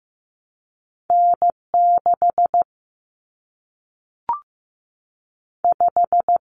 Morse code, heard as a keyed tone, about 15 wpm, 700 Hz. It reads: N6 5
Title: Common Cut Numbers - 2-Digit 15wpm